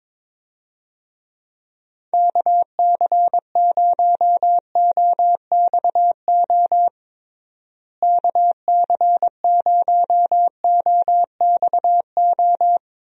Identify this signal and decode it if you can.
Morse code, heard as a keyed tone, about 22 wpm, 700 Hz. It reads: KC0OXO KC0OXO
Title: USA Call Signs - 6 Characters - Repeated 2x - 22wpm